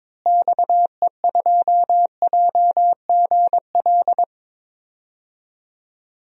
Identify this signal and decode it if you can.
Morse code, heard as a keyed tone, about 22 wpm, 700 Hz. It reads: XE2JGL
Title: DXCC-100 Call Signs - 6 Characters - Repeated 3x - 22wpm